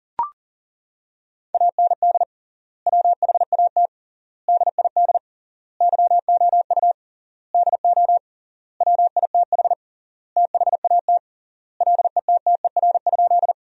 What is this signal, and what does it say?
Morse code, heard as a keyed tone, about 40 wpm, 700 Hz. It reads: AND WHAT DID YOU DO WITH THAT LETTER?
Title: 3x Word-Spacing - Sentences from Top 300 Words - 40wpm